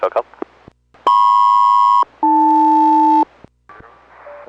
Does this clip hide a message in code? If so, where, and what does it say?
Yes: aircraft SELCAL from 1.1 to 3.2 s, heard as two pairs of tones, with speech. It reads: MP-AL